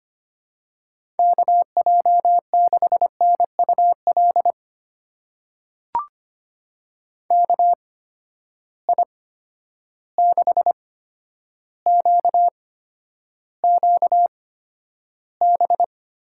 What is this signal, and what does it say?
Morse code, heard as a keyed tone, about 25 wpm, spaced as for 11 wpm, 700 Hz. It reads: KJ6NUL KI6QQB
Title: USA Call Signs - 6 Characters - 8x Character Spacing - 25wpm